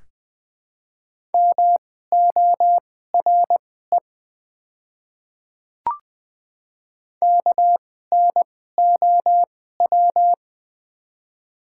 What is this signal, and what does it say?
Morse code, heard as a keyed tone, about 20 wpm, spaced as for 15 wpm, 700 Hz. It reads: MORE KNOW
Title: 2x Character Spacing - Top 200 Words - 20wpm